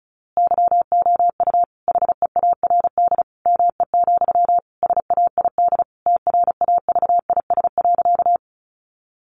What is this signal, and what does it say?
Morse code, heard as a keyed tone, about 34 wpm, 700 Hz. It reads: YOU HEARD ME, SAID TRAVIS.
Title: A Sound Of Thunder 35wpm